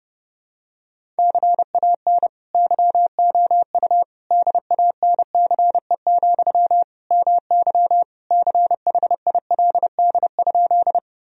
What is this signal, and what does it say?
Morse code, heard as a keyed tone, about 30 wpm, 700 Hz. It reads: CAN YOU DANCE, MY CHILD?
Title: Sentences from Top 1000 Words - Review Part 4 - 30wpm